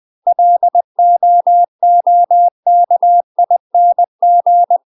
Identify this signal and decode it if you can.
Morse code, heard as a keyed tone, about 20 wpm, 700 Hz. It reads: LOOKING
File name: UK MASTER QSOs TRAINING / 20WPM / UK MASTER QSOs 0009